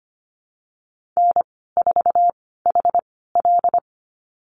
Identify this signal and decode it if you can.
Morse code, heard as a keyed tone, about 25 wpm, spaced as for 16 wpm, 700 Hz. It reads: N4HL
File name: Lesson 15g - Number 5 - Call Signs 25wpm